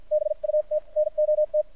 Morse code, heard as a keyed tone, about 52 wpm, spaced as for 43 wpm, 610 Hz. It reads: BUT NOT
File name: archived / CRSnet